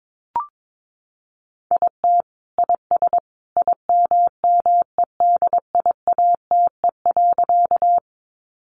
Morse code, heard as a keyed tone, about 22 wpm, 700 Hz. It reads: IT IS IMMEDIATE.
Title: Sentences from Top 1200 Words 22wpm